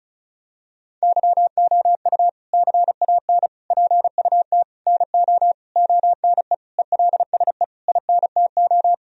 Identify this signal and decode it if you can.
Morse code, heard as a keyed tone, about 34 wpm, 700 Hz. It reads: YOU CAN PUT NO ONE ELSE INTO
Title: Sentences from Top 1000 Words - Review Part 2 - 35wpm